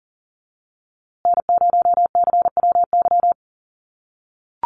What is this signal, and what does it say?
Morse code, heard as a keyed tone, about 40 wpm, 700 Hz. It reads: N0CWY
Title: USA Call Signs Repeated 3x 40wpm